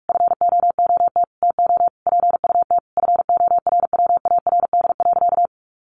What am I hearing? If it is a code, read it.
Morse code, heard as a keyed tone, about 44 wpm, 700 Hz. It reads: FOOT TO PUT FORWARD.
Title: Sentences from Top 1300 Words 45wpm